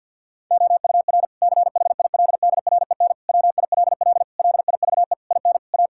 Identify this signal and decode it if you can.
Morse code, heard as a keyed tone, about 50 wpm, 700 Hz. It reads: OUR CHILDREN WILL LIVE IN A
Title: Sentences from Top 1000 Words - Review Part 5 - 50wpm